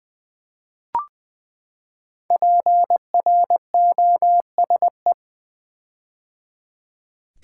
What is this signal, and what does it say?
Morse code, heard as a keyed tone, about 20 wpm, 700 Hz. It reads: PROSE
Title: Build A Word - 5 Letters At A Time - 10-Letter Words - 20wpm